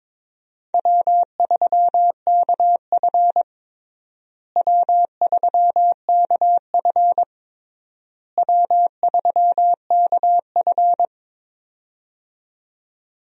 Morse code, heard as a keyed tone, about 22 wpm, 700 Hz. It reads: W3KF W3KF W3KF
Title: USA Call Signs - 4 Characters - Repeated 3x - 22wpm